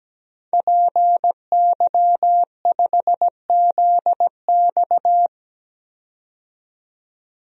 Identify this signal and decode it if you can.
Morse code, heard as a keyed tone, about 17 wpm, 700 Hz. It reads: PY5ZX
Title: DXCC-100 Call Signs - 5 Characters - 17wpm